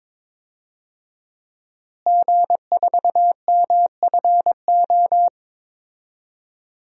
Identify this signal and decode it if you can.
Morse code, heard as a keyed tone, about 22 wpm, 700 Hz. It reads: G4MFO
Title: DXCC-100 Call Signs - 5 Characters - Repeated 2x - 22wpm